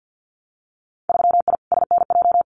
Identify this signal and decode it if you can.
Morse code, heard as a keyed tone, about 50 wpm, 700 Hz. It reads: 3I SNJ
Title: Winter Field Day Exchanges Encore 50wpm